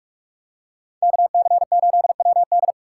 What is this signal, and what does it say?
Morse code, heard as a keyed tone, about 44 wpm, 700 Hz. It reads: KC8WD